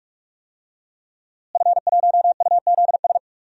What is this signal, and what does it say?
Morse code, heard as a keyed tone, about 46 wpm, 700 Hz. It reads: F1UZS
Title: DXCC-100 Call Signs - 5 Characters - Repeated 3x - 45wpm